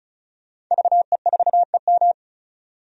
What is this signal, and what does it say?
Morse code, heard as a keyed tone, about 36 wpm, 700 Hz. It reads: VE4EM
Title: DXCC-100 Call Signs - 5 Characters - 35wpm